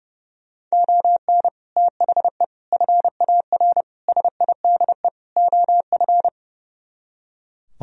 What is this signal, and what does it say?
Morse code, heard as a keyed tone, about 36 wpm, 700 Hz. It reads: ON THE FAR SIDE OF
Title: Speed-Racing - Sets of 5 Words 30wpm